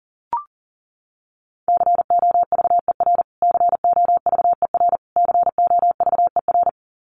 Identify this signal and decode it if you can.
Morse code, heard as a keyed tone, about 40 wpm, 700 Hz. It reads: COVER COVER COVER
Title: Top 300 Words Repeated 40wpm